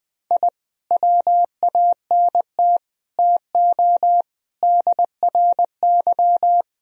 Morse code, heard as a keyed tone, about 20 wpm, 700 Hz. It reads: I WANT TO DRY
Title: Sentences from Top 1000 Words - Review Part 3 - 20wpm